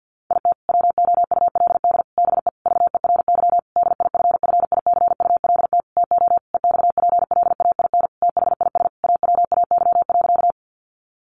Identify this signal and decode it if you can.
Morse code, heard as a keyed tone, about 50 wpm, 700 Hz. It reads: IT WOULD BE VERY DIFFICULT TO EXPLAIN THIS AWAY.
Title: Sentences from Top 1100 Words 50wpm